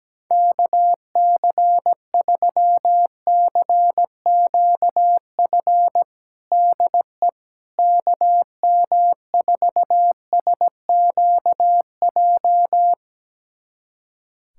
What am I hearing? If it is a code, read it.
Morse code, heard as a keyed tone, about 17 wpm, 700 Hz. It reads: KC3CQF DE KM4SQJ
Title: USA Callsign DE Callsign - 6 Characters - 17wpm